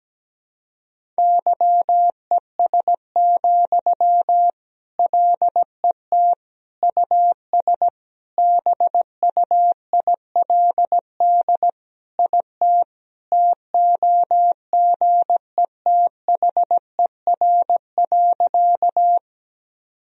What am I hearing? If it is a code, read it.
Morse code, heard as a keyed tone, about 17 wpm, 700 Hz. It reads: YES, LET US BUILD IT TOGETHER.
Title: Sentences from Top 1000 Words - Review Part 1 - 17wpm